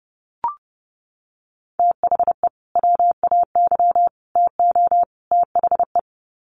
Morse code, heard as a keyed tone, about 30 wpm, 700 Hz. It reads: THE WAY TO THE
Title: Sets of 4 Words 30wpm